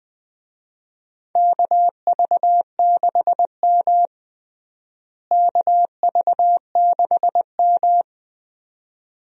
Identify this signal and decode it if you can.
Morse code, heard as a keyed tone, about 20 wpm, 700 Hz. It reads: KV6M KV6M
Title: USA Call Signs - 4 Characters - Repeated 2x - 20wpm